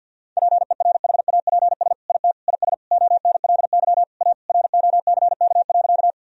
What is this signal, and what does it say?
Morse code, heard as a keyed tone, about 50 wpm, 700 Hz. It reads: PERHAPS IT IS ONLY A ROCK.
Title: Sentences from Top 1000 Words - Review Part 2 - 50wpm